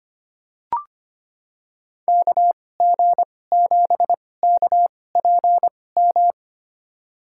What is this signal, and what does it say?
Morse code, heard as a keyed tone, about 25 wpm, spaced as for 18 wpm, 690 Hz. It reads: KG7KPM
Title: USA Call Signs - 6 Characters - 2x Character Spacing - 25wpm